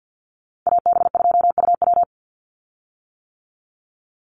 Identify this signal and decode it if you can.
Morse code, heard as a keyed tone, about 50 wpm, 700 Hz. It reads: AB2UW
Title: USA Call Signs 50wpm